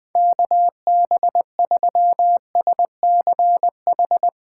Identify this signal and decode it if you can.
Morse code, heard as a keyed tone, about 20 wpm, 700 Hz. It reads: KB3SCH
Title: Speed Racing - USA 2x3 Call Signs 20wpm